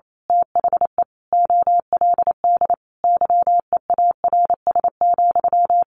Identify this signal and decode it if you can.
Morse code, heard as a keyed tone, about 28 wpm, 700 Hz. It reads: THE OLD YEARS,
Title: A Sound Of Thunder 28wpm